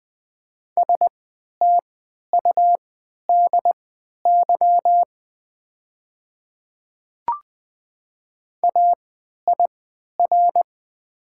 Morse code, heard as a keyed tone, about 20 wpm, spaced as for 11 wpm, 700 Hz. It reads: STUDY AIR
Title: 3x Character Spacing - Top 300 Words - 20wpm